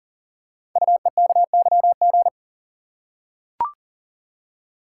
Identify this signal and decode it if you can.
Morse code, heard as a keyed tone, about 39 wpm, 700 Hz. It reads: UEKYG
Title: Code Groups - 5 Letters - 40wpm